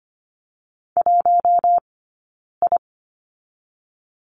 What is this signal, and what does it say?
Morse code, heard as a keyed tone, about 25 wpm, 700 Hz. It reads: 1 I
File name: course / 25:10wpm / 092 Lesson 16d - Letter c - Two Characters 25wpm